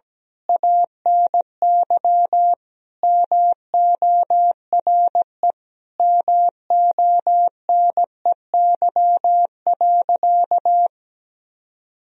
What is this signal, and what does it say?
Morse code, heard as a keyed tone, about 17 wpm, 700 Hz. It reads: ANY MORE MONEY.
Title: Sentences from Top 1000 Words - Review Part 4 - 17wpm